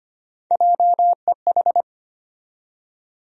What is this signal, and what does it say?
Morse code, heard as a keyed tone, about 25 wpm, 700 Hz. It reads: JEH
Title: USA Call Sign Postfixes - 3 Letters 25wpm